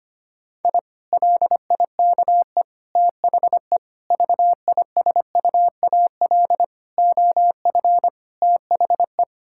Morse code, heard as a keyed tone, about 25 wpm, 700 Hz. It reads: I LIKE THE VISUAL OF THE
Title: Sentences from Top 1000 Words - Review Part 6 - 25wpm